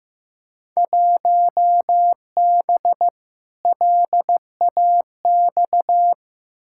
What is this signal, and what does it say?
Morse code, heard as a keyed tone, about 15 wpm, 700 Hz. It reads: 1B LAX
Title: Field Day Exchanges - 15wpm